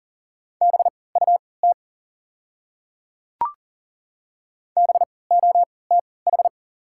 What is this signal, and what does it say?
Morse code, heard as a keyed tone, about 40 wpm, spaced as for 23 wpm, 700 Hz. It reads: BUT BOTH
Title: 3x Character Spacing - Top 300 Words - 40wpm